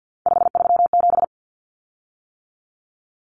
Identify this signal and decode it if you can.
Morse code, heard as a keyed tone, about 48 wpm, 700 Hz. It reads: 537
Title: Common Cut Numbers - 3-Digit 50wpm